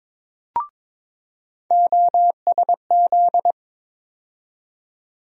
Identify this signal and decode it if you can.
Morse code, heard as a keyed tone, about 21 wpm, 700 Hz. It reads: OSZ
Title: USA Call Sign Postfixes - 3 Letters 22wpm